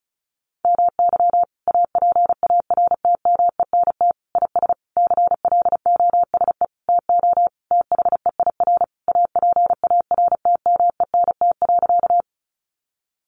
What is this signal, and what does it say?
Morse code, heard as a keyed tone, about 34 wpm, 700 Hz. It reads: MY APARTMENT IS CLOSE TO THEIR APARTMENT.
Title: Sentences from Top 900 Words 35wpm